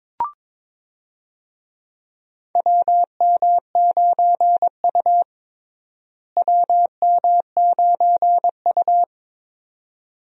Morse code, heard as a keyed tone, about 21 wpm, 700 Hz. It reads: WM9U WM9U
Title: USA Call Signs - 4 Characters - Repeated 2x - 22wpm